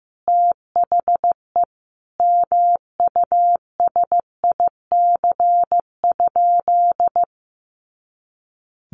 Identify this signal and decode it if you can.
Morse code, heard as a keyed tone, about 15 wpm, 700 Hz. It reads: THE MUSIC?
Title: Sentences from Top 300 Words 15wpm